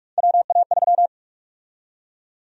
Morse code, heard as a keyed tone, about 46 wpm, 700 Hz. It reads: WA3
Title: USA Call Sign Prefixes - 45wpm